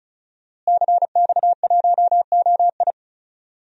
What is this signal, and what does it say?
Morse code, heard as a keyed tone, about 36 wpm, 700 Hz. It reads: CX1OI
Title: DXCC-100 Call Signs - 5 Characters - Repeated 2x - 35wpm